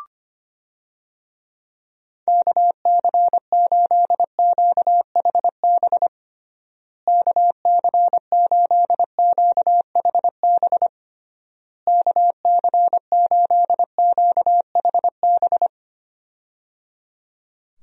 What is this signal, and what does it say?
Morse code, heard as a keyed tone, about 25 wpm, 700 Hz. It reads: KC8QHB KC8QHB KC8QHB
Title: USA Call Signs - 6 Characters - Repeated 3x - 25wpm